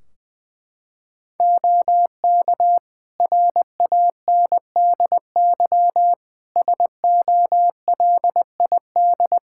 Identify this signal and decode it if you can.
Morse code, heard as a keyed tone, about 20 wpm, 700 Hz. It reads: OK RANDY SOLID
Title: Practice QSO - Exchange 3 - More Info - 20wpm